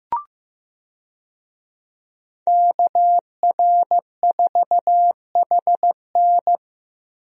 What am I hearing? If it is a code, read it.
Morse code, heard as a keyed tone, about 15 wpm, 700 Hz. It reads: KR4HN
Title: USA Call Signs - 5 Characters - Repeated 3x - 15wpm